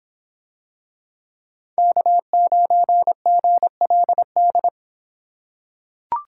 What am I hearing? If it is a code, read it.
Morse code, heard as a keyed tone, about 25 wpm, 700 Hz. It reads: K9GLD